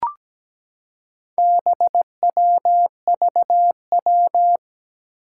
Morse code, heard as a keyed tone, about 17 wpm, 700 Hz. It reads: BWVW
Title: Four Challenging Letter-Number Combinations 17wpm